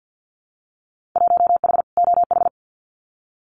Morse code, heard as a keyed tone, about 50 wpm, 700 Hz. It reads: 1H OH
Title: Winter Field Day Exchanges 50wpm